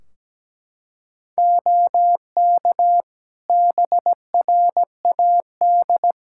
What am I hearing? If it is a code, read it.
Morse code, heard as a keyed tone, about 17 wpm, 700 Hz. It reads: OK BRAD
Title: K1USN SST QSO Practice 17wpm